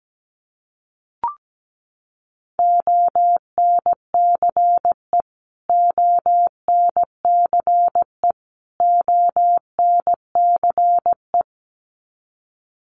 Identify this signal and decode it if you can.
Morse code, heard as a keyed tone, about 17 wpm, 700 Hz. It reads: ONCE ONCE ONCE
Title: Top 500 Words Repeated 17wpm